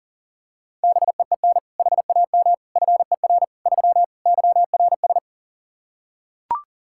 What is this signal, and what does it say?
Morse code, heard as a keyed tone, about 40 wpm, 700 Hz. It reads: BEEN HAM FER 3 YRS
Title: K1USN SST QSO Practice 40wpm